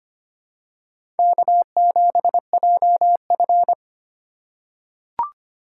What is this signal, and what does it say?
Morse code, heard as a keyed tone, about 25 wpm, 700 Hz. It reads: K7JF
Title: USA Call Signs - 2x Character Spacing - 25wpm